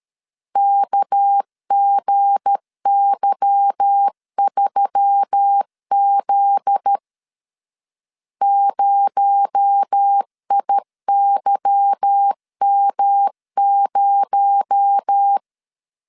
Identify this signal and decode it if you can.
Morse code, heard as a keyed tone, about 13 wpm, 800 Hz. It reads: KGY3Z 0IYM0
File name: LEZIONI / 8 YJ45 / 8 Y 12wpm 2228